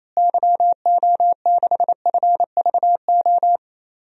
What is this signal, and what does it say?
Morse code, heard as a keyed tone, about 28 wpm, 700 Hz. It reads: YO6FVO
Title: DXCC-100 Call Signs - 6 Characters - Repeated 3x - 28wpm